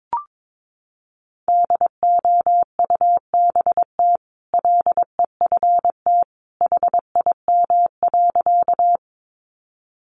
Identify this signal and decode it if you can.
Morse code, heard as a keyed tone, about 22 wpm, 700 Hz. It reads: DOUBT LEFT HIM.
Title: Sentences from Top 1500 Words 22wpm